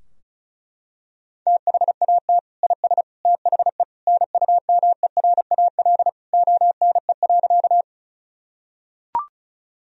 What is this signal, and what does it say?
Morse code, heard as a keyed tone, about 34 wpm, 700 Hz. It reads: THAT IS THE NUMERAL ONE.
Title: Sentences from Top 1000 Words - Review Part 2 - 35wpm